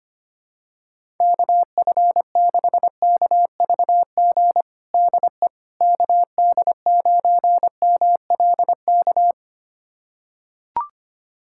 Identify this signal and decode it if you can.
Morse code, heard as a keyed tone, about 25 wpm, 700 Hz. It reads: KF6KVG DE KD9MLK
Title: USA Callsign DE Callsign 25wpm